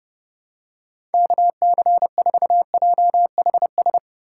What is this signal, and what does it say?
Morse code, heard as a keyed tone, about 29 wpm, 700 Hz. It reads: KC4JHS
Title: USA Call Signs - 6 Characters - 4x Character Spacing - 30wpm